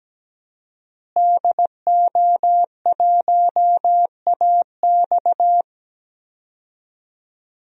Morse code, heard as a keyed tone, about 17 wpm, 700 Hz. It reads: DO1AX
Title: DXCC-100 Call Signs - 5 Characters - Repeated 3x - 17wpm